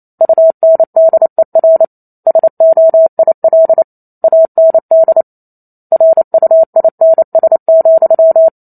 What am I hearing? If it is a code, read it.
Morse code, heard as a keyed tone, about 28 wpm, 650 Hz. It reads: UNDER SOIL AND RUINS,
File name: Chapter 2 - 28 WPM